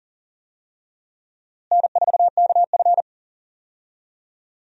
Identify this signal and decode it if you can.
Morse code, heard as a keyed tone, about 40 wpm, 700 Hz. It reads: N4KF
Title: USA Call Signs - 4 Characters - Repeated 2x - 40wpm